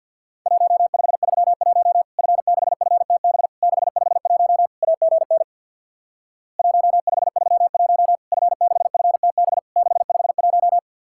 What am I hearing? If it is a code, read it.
Morse code, heard as a keyed tone, about 50 wpm, 700 Hz. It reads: 1531 F6FTB 651 AGN 1531 F6FTB 651
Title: Worked All Europe DX Contest - QTC Exchange with Repeats - 50wpm